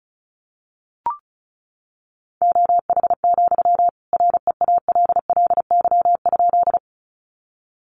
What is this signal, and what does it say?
Morse code, heard as a keyed tone, about 34 wpm, 700 Hz. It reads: OH, REALLY?
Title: Sentences from Top 600 Words 35wpm